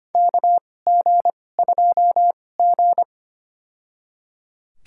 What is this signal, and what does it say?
Morse code, heard as a keyed tone, about 25 wpm, spaced as for 18 wpm, 700 Hz. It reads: KG2G
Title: USA Call Signs - 4 Characters - 2x Character Spacing - 25wpm